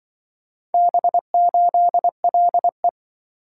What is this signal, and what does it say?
Morse code, heard as a keyed tone, about 24 wpm, 700 Hz. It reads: B8LE
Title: Speed-Racing - Four Letter-Number Combinations - 20wpm